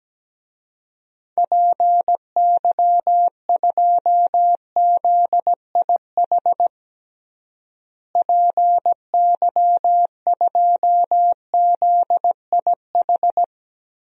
Morse code, heard as a keyed tone, about 17 wpm, 700 Hz. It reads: PY2ZIH PY2ZIH
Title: DXCC-100 Call Signs - 6 Characters - Repeated 3x - 17wpm